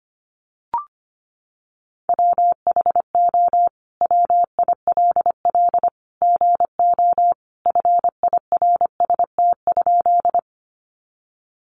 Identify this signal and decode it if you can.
Morse code, heard as a keyed tone, about 25 wpm, 700 Hz. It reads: WHO WILL GO FIRST?
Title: Sentences from Top 100 Words 25wpm